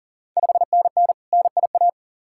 Morse code, heard as a keyed tone, about 44 wpm, 690 Hz. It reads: 5NN NIA
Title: Ontario QSO Party Exchange 40wpm